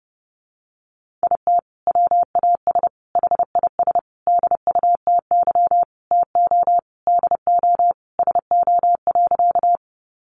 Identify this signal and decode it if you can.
Morse code, heard as a keyed tone, about 30 wpm, 700 Hz. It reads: IT WAS HIS DUTY TO DO SO.
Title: Sentences from Top 1500 Words 30wpm